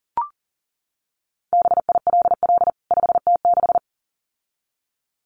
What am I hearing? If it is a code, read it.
Morse code, heard as a keyed tone, about 40 wpm, 700 Hz. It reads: BILL 5T6
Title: CWT Exchange 40wpm